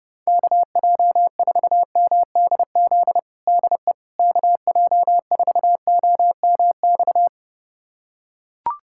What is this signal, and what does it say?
Morse code, heard as a keyed tone, about 30 wpm, 700 Hz. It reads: KJ4MDZ DE KJ4OMX
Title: USA Callsign DE Callsign - 6 Characters - 30wpm